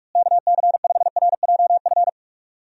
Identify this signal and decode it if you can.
Morse code, heard as a keyed tone, about 46 wpm, 700 Hz. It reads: KC5RJF
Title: USA Call Signs - 6 Characters - Repeated 2x - 45wpm